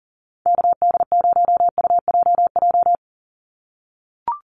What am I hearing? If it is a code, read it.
Morse code, heard as a keyed tone, about 40 wpm, 690 Hz. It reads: KD0UJJ
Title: USA Call Signs 40wpm